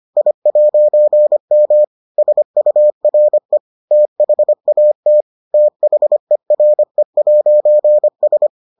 Morse code, heard as a keyed tone, about 25 wpm, 600 Hz. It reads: I'M SURE THAT THERE'S